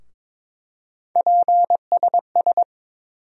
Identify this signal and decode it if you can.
Morse code, heard as a keyed tone, about 22 wpm, 700 Hz. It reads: PSS
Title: USA Call Sign Postfixes - 3 Letters 22wpm